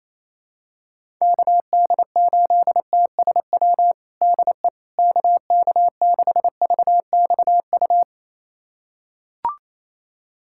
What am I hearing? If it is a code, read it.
Morse code, heard as a keyed tone, about 28 wpm, 700 Hz. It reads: KD8TSW DE KK6VXU
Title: USA Callsign DE Callsign - 6 Characters - 28wpm